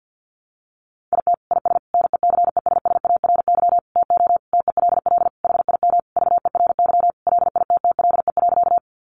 Morse code, heard as a keyed tone, about 50 wpm, 700 Hz. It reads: IT IS NECESSARY TO TELL HIM VERY LITTLE.
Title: Sentences from Top 800 Words 50wpm